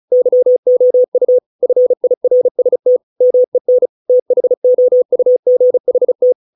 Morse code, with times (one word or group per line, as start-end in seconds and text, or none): 0.1-1.4 YOU
1.6-3.0 FIRST
3.2-3.9 MEN
4.1-6.3 THOUGHT